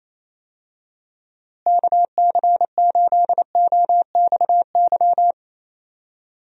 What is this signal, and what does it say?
Morse code, heard as a keyed tone, about 28 wpm, 700 Hz. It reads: KC8OXY
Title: USA Call Signs - 6 Characters - Repeated 3x - 28wpm